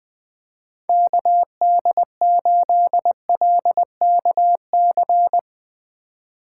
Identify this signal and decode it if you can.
Morse code, heard as a keyed tone, about 21 wpm, 700 Hz. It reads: KD8LKC
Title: USA Call Signs - 4x Character Spacing - 20wpm